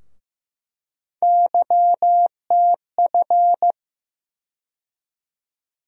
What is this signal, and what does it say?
Morse code, heard as a keyed tone, about 15 wpm, 700 Hz. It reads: YTF
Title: USA Call Sign Postfixes - 3 Letters 15wpm